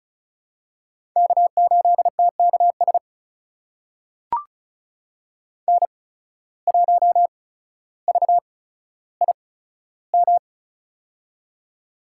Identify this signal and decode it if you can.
Morse code, heard as a keyed tone, about 36 wpm, spaced as for 15 wpm, 700 Hz. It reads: K8TKS N1VIM
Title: USA Call Signs - 5 Characters - 8x Character Spacing - 35wpm